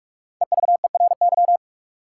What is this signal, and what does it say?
Morse code, heard as a keyed tone, about 46 wpm, 700 Hz. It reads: EVERY
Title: Sentences from Top 200 Words - New Word Drill - 45wpm